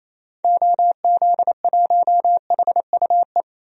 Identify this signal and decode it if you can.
Morse code, heard as a keyed tone, about 28 wpm, 700 Hz. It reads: OZ1HUE
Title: DXCC-100 Call Signs - 6 Characters - 28wpm